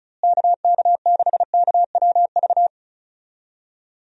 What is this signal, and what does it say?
Morse code, heard as a keyed tone, about 36 wpm, 700 Hz. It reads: KK6KWV
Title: USA Call Signs - 6 Characters - Repeated 3x - 35wpm